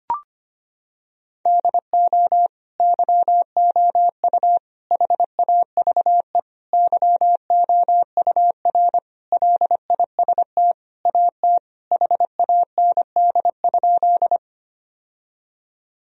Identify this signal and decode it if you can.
Morse code, heard as a keyed tone, about 25 wpm, 700 Hz. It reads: DO YOU HAVE YOUR LIST AT HAND?
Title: Sentences from Top 1000 Words - Review Part 2 - 25wpm